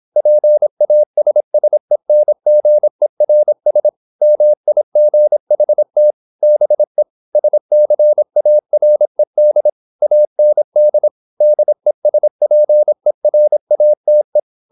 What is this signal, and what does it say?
Morse code, heard as a keyed tone, about 26 wpm, 600 Hz. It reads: PASSENGERS MIGHT BE SCARED AND DESPERATE
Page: Never 26wpm | CW med Gnister